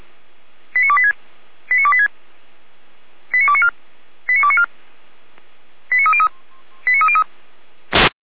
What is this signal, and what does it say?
Five-tone selective call (ZVEI), heard as quick runs of five tones, tones of about 70 ms. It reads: 89286 89286 89294 89294 89303 89303